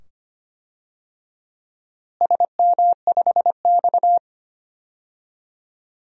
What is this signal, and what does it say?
Morse code, heard as a keyed tone, about 25 wpm, 700 Hz. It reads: SM5X
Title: DXCC-100 Call Signs - 4 Characters - Repeated 2x - 25wpm